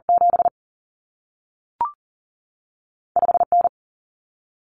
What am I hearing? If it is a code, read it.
Morse code, heard as a keyed tone, about 40 wpm, spaced as for 3 wpm, 700 Hz. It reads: 75 N
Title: Common Cut Numbers - 2-Digit 40wpm